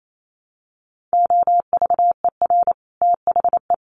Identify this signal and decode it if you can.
Morse code, heard as a keyed tone, about 28 wpm, 700 Hz. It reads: OVER THE
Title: Sets of 5 Words Encore2 28wpm